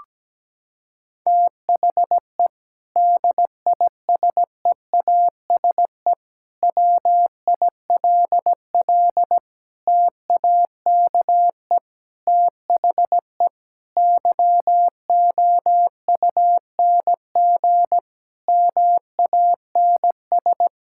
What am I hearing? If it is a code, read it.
Morse code, heard as a keyed tone, about 17 wpm, 700 Hz. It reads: THE DISEASE WILL TAKE THE YOUNG MANS
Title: Sentences from Top 1000 Words - Review Part 6 - 17wpm